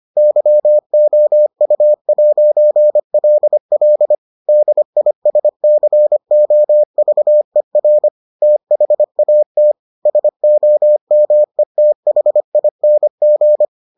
Morse code, heard as a keyed tone, about 25 wpm, 600 Hz. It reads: YOU'LL DISCOVER THAT SOMETHING